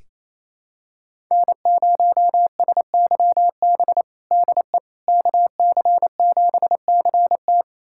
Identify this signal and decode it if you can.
Morse code, heard as a keyed tone, about 28 wpm, 700 Hz. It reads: N0SYB DE KC7CT
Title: USA Callsign DE Callsign - 5 Characters - 28wpm